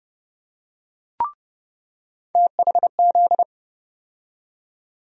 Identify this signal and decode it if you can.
Morse code, heard as a keyed tone, about 30 wpm, 700 Hz. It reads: THZ
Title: USA Call Sign Postfixes - 3 Letters 30wpm